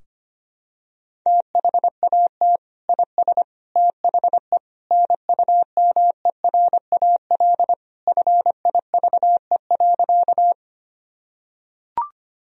Morse code, heard as a keyed tone, about 25 wpm, 700 Hz. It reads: THAT IS THE NUMERAL FIVE.
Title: Sentences from Top 1000 Words - Review Part 2 - 25wpm